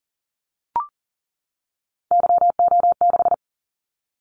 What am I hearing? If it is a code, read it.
Morse code, heard as a keyed tone, about 40 wpm, 700 Hz. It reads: YO6
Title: Callsign Prefixes for Top 25 DXCC - 40wpm